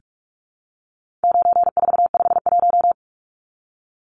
Morse code, heard as a keyed tone, about 44 wpm, 700 Hz. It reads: 9451
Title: Numbers - 4-Digit 45wpm